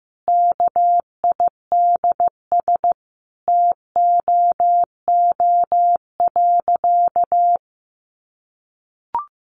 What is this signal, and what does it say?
Morse code, heard as a keyed tone, about 15 wpm, 700 Hz. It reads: KIDS TOO.
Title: Easy Phrases 15wpm